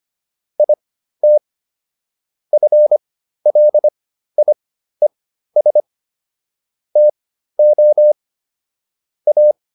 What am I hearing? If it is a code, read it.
Morse code, heard as a keyed tone, about 25 wpm, spaced as for 13 wpm, 600 Hz. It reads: IT FLIES TO A